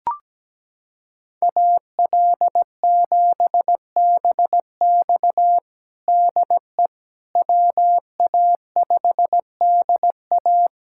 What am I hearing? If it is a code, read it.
Morse code, heard as a keyed tone, about 17 wpm, 700 Hz. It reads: AL7BX DE WA5DA
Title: USA Callsign DE Callsign - 5 Characters - 17wpm